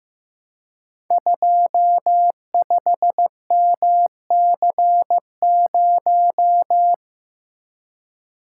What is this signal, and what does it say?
Morse code, heard as a keyed tone, about 15 wpm, 700 Hz. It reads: 25MC0
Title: Code Groups - 5 Letter:Numbers - 15wpm